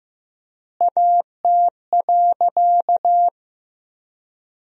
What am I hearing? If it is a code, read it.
Morse code, heard as a keyed tone, about 15 wpm, 700 Hz. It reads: AT.
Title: 4x Word-Spacing - Sentences from Top 200 Words - 15wpm